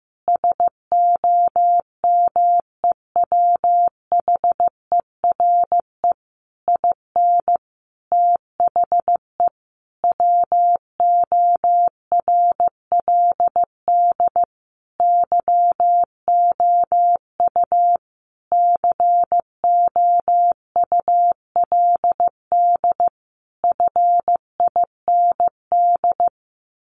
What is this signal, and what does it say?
Morse code, heard as a keyed tone, about 15 wpm, 700 Hz. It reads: SOMEWHERE IN THE WORLD YOU COULD FIND
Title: Sentences from Top 1400 Words 15wpm